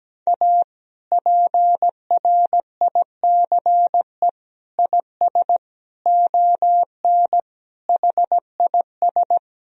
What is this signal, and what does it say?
Morse code, heard as a keyed tone, about 17 wpm, 700 Hz. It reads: A PRICE IS ON HIS
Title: Sentences from Top 1000 Words - Review Part 4 - 17wpm